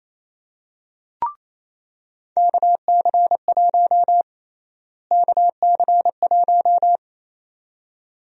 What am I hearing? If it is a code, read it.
Morse code, heard as a keyed tone, about 28 wpm, 700 Hz. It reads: KC1 KC1
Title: USA Call Sign Prefixes - Repeated 2x - 28wpm